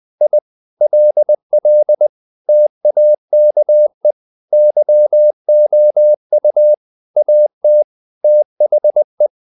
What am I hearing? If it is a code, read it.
Morse code, heard as a keyed tone, about 20 wpm, 600 Hz. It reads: I LL TAKE YOU AT THE
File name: the old man and the sea part 2 20wpm